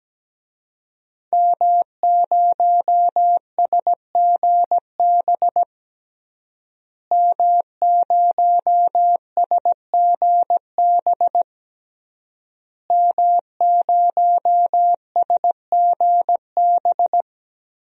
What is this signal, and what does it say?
Morse code, heard as a keyed tone, about 17 wpm, 700 Hz. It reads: M0SGB M0SGB M0SGB
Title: DXCC-100 Call Signs - 5 Characters - Repeated 3x - 17wpm